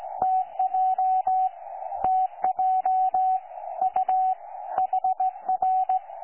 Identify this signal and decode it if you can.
Morse code, heard as a keyed tone, about 18 wpm, 750 Hz, with radio noise. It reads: TJ TJ U HR
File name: 12 39Z 18088.2kHz utah aa7fv k9tjl